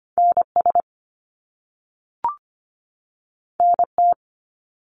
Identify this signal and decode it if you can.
Morse code, heard as a keyed tone, about 25 wpm, 700 Hz. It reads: NS NT
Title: Canadian Provinces and Territories 25wpm